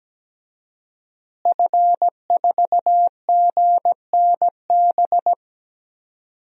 Morse code, heard as a keyed tone, about 17 wpm, 700 Hz. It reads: F4GNB
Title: DXCC-100 Call Signs - 5 Characters - Repeated 2x - 17wpm